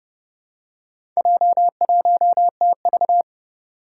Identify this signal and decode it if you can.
Morse code, heard as a keyed tone, about 30 wpm, 700 Hz. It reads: J1TV
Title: Four Challenging Letter-Number Combinations 30wpm